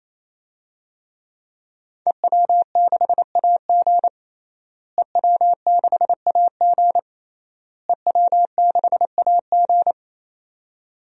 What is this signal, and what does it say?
Morse code, heard as a keyed tone, about 28 wpm, 690 Hz. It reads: EW6AG EW6AG EW6AG
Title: DXCC-100 Call Signs - 5 Characters - Repeated 3x - 28wpm